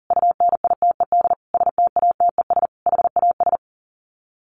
Morse code, heard as a keyed tone, about 40 wpm, 700 Hz. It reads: UNITED STATES HAS
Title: Sets of 5 Words Encore 40wpm